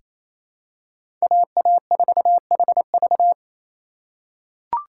Morse code, heard as a keyed tone, about 28 wpm, 700 Hz. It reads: AA4HV
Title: K1USN SST QSO Practice - 28wpm